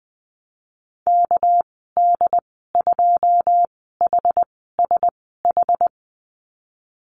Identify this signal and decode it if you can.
Morse code, heard as a keyed tone, about 20 wpm, 700 Hz. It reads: K D 2 H S H
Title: 200 Lesson 32g - Letter k - Call Signs 20wpm